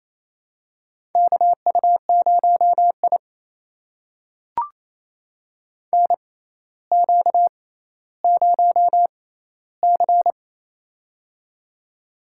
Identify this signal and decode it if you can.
Morse code, heard as a keyed tone, about 29 wpm, spaced as for 15 wpm, 700 Hz. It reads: KU0I NQ0C
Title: USA Call Signs - 4 Characters - 6x Character Spacing - 28wpm